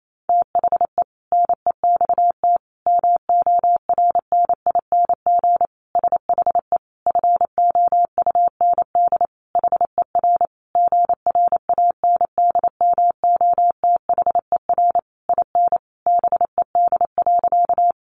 Morse code, heard as a keyed tone, about 28 wpm, 700 Hz. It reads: THE NEXT MORNING SHE FOUND HER GRANDMOTHER IN BED.
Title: Sentences from Top 1200 Words 28wpm